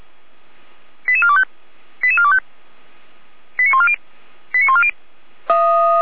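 Five-tone selective call (ZVEI), heard as quick runs of five tones, tones of about 70 ms. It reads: 88425 88425 89140 89140
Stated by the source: ZVEI five-tone call 88393 received on SDR.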